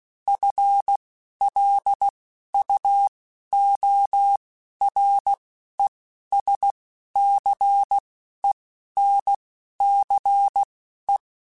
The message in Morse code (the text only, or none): FLUORESCENCE